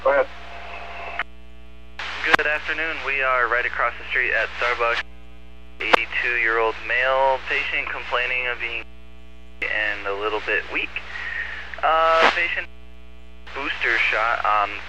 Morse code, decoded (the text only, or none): none